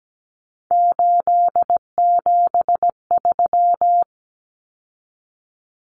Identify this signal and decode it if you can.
Morse code, heard as a keyed tone, about 17 wpm, 700 Hz. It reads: 873
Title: Common Cut Numbers - 3-Digit 17wpm